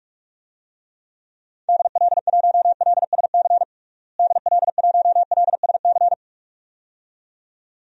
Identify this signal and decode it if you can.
Morse code, heard as a keyed tone, about 46 wpm, 700 Hz. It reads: DL1LSC DL1LSC
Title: DXCC-100 Call Signs - 6 Characters - Repeated 2x - 45wpm